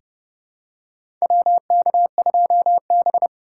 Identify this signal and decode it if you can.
Morse code, heard as a keyed tone, about 30 wpm, 700 Hz. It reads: WK2B
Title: USA Call Signs - 4x Character Spacing - 30wpm